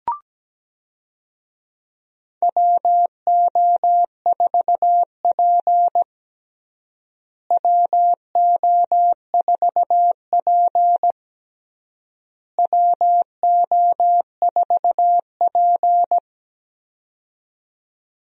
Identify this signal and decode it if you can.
Morse code, heard as a keyed tone, about 17 wpm, 700 Hz. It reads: WO4P WO4P WO4P
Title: USA Call Signs - 4 Characters - Repeated 3x - 17wpm